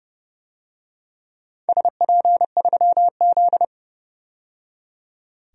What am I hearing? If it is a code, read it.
Morse code, heard as a keyed tone, about 31 wpm, 700 Hz. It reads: SP3Z
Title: DXCC-100 Call Signs - 4 Characters - 30wpm